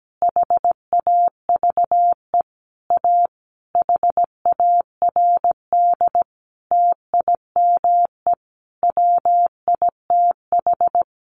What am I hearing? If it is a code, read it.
Morse code, heard as a keyed tone, about 17 wpm, 700 Hz. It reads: HAVE A HARD TIME WITH
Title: Sentences from Top 300 Words 17wpm